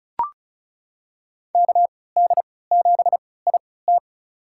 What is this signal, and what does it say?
Morse code, heard as a keyed tone, about 36 wpm, spaced as for 20 wpm, 700 Hz. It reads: KD7IT
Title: USA Call Signs - 3x Character Spacing - 35wpm